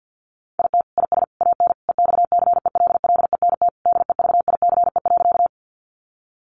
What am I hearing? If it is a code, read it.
Morse code, heard as a keyed tone, about 50 wpm, 700 Hz. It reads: IT IS AN EXCELLENT DEVICE.
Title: Sentences from Top 1500 Words 50wpm